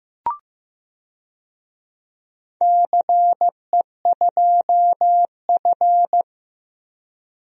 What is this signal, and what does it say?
Morse code, heard as a keyed tone, about 15 wpm, 700 Hz. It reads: CE2F
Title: DXCC-100 Call Signs - 4 Characters - Repeated 3x - 15wpm